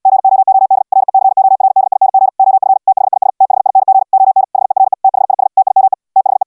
Morse code, heard as a keyed tone, about 69 wpm, 770 Hz. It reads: GOOD RECORDING OF THIS THING ON HERE THIS TIME THE